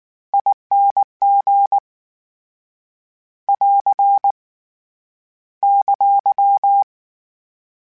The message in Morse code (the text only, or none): ING <AR> !